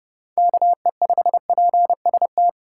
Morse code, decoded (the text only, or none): KE5PST